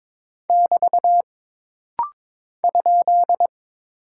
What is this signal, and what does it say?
Morse code, heard as a keyed tone, about 22 wpm, 700 Hz. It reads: <BT> ?
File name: practice / 22wpm / Single Punctuation-Prosign - Rapid-Fire - 22wpm